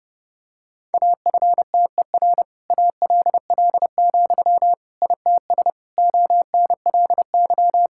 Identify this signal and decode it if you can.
Morse code, heard as a keyed tone, about 30 wpm, 700 Hz. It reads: AFTER ALL, ITS ONLY
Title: Sentences from Top 1000 Words - Review Part 5 - 30wpm